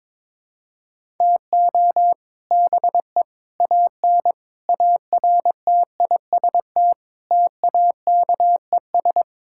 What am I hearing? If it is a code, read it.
Morse code, heard as a keyed tone, about 22 wpm, 700 Hz. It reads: TO BE AN ARTIST TAKES
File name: practice / 22wpm / Sentences with 3-Word Phrases - 22wpm